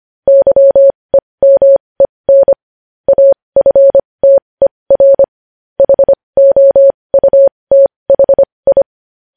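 Morse code, heard as a keyed tone, about 25 wpm, spaced as for 20 wpm, 550 Hz. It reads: YEMEN AFTER HOUTHI
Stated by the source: Morse practice of news headlines at 20 wpm.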